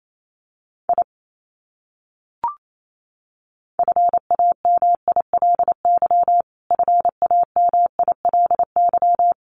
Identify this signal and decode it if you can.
Morse code, heard as a keyed tone, about 28 wpm, 700 Hz. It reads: I FAMILY FAMILY
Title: Top 500 Words Repeated 28wpm